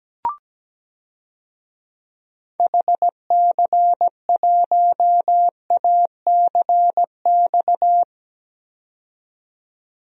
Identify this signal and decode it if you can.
Morse code, heard as a keyed tone, about 17 wpm, 700 Hz. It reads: HC1ACX
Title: DXCC-100 Call Signs - 6 Characters - 17wpm